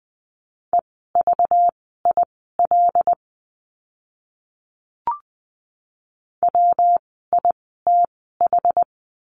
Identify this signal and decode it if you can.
Morse code, heard as a keyed tone, about 20 wpm, spaced as for 15 wpm, 700 Hz. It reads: EVIL WITH
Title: 192 Lesson 31f - Letter v - Words 20wpm